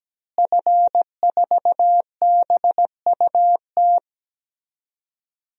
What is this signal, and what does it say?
Morse code, heard as a keyed tone, about 17 wpm, 700 Hz. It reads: F4BUT
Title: DXCC-100 Call Signs - 5 Characters - Repeated 3x - 17wpm